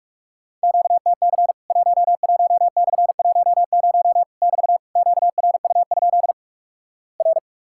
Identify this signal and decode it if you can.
Morse code, heard as a keyed tone, about 45 wpm, 700 Hz. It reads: QTC 11/10 <BT> QRU? R
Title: Worked All Europe DX Contest - QTC Exchange - 45wpm